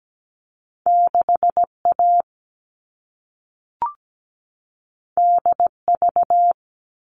Morse code, heard as a keyed tone, about 17 wpm, 700 Hz. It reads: 6A DV
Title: All Cut Numbers - 2-Digit 17wpm